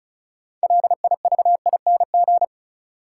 Morse code, heard as a keyed tone, about 36 wpm, 700 Hz. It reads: LIVING
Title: Sets of 4 Words - New Word Drill - 35wpm